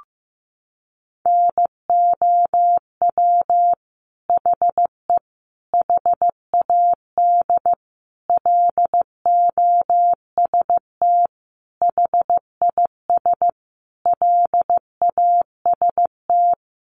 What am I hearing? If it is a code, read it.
Morse code, heard as a keyed tone, about 15 wpm, 700 Hz. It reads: NOW HE HAD LOST HIS LAST
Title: Sentences from Top 1400 Words 15wpm